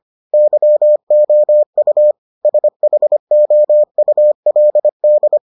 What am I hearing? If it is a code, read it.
Morse code, heard as a keyed tone, about 25 wpm, 600 Hz. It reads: YOU SHOULD